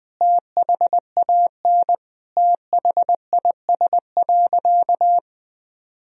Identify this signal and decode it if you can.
Morse code, heard as a keyed tone, about 20 wpm, 700 Hz. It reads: THAN THIS.
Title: Sentences from Top 1000 Words - Review Part 3 - 20wpm